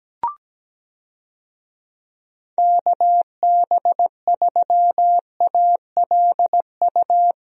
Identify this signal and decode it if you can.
Morse code, heard as a keyed tone, about 17 wpm, 700 Hz. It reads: KB3ALU
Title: USA Call Signs - 6 Characters - 17wpm